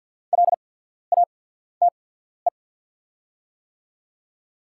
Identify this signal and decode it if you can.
Morse code, heard as a keyed tone, about 48 wpm, spaced as for 14 wpm, 690 Hz. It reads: LATE
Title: 8x Character Spacing - Top 300 Words - 50wpm